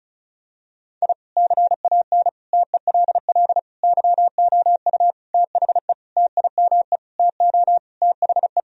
Morse code, heard as a keyed tone, about 34 wpm, 690 Hz. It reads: I CAN TELL YOU THE TIME TO THE
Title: Sentences from Top 1000 Words - Review Part 1 - 35wpm